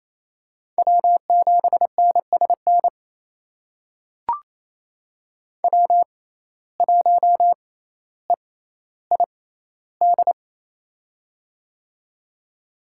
Morse code, heard as a keyed tone, about 28 wpm, spaced as for 14 wpm, 700 Hz. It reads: W7NSN W1EID
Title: USA Call Signs - 5 Characters - 6x Character Spacing - 28wpm